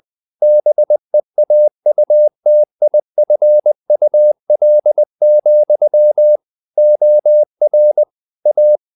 Morse code, heard as a keyed tone, about 20 wpm, 600 Hz. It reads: BEAUTIFUL, OR A
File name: the old man and the sea part 2 20wpm